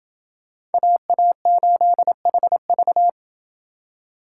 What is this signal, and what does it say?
Morse code, heard as a keyed tone, about 29 wpm, 700 Hz. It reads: AA8HV